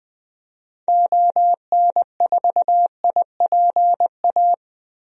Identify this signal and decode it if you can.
Morse code, heard as a keyed tone, about 20 wpm, 700 Hz. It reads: ON4IPA
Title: DXCC-100 Call Signs - 6 Characters - Repeated 3x - 20wpm